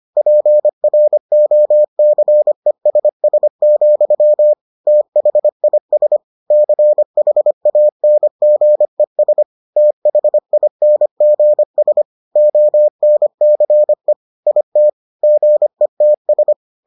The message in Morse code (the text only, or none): PROCESS, THIS CHANGES THINGS ONCE IT GETS